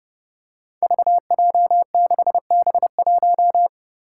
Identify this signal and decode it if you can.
Morse code, heard as a keyed tone, about 30 wpm, 700 Hz. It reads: VJ6B1
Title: Five Challenging Letter-Number Combinations 30wpm